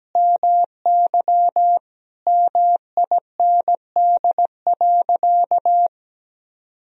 Morse code, heard as a keyed tone, about 17 wpm, 700 Hz. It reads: MY MIND.
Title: Sentences from Top 1000 Words - Review Part 1 - 17wpm